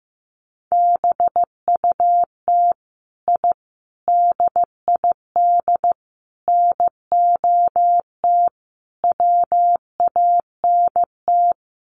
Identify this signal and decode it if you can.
Morse code, heard as a keyed tone, about 15 wpm, 700 Hz. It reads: BUT I DID NOT WANT
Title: Sets of 5 Words Encore 15wpm